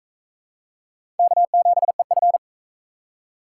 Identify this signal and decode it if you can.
Morse code, heard as a keyed tone, about 44 wpm, 700 Hz. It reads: K7EF